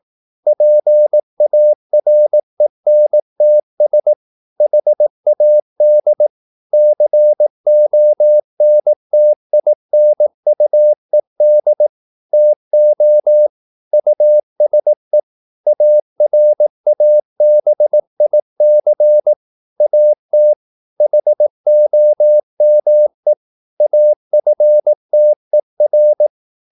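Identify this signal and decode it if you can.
Morse code, heard as a keyed tone, about 18 wpm, 600 Hz. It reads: PARENTS HAD CONTINUED TO USE ARABIC AT HOME AFTER
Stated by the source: Morse practice at 18 wpm, English text, from CW med Gnister.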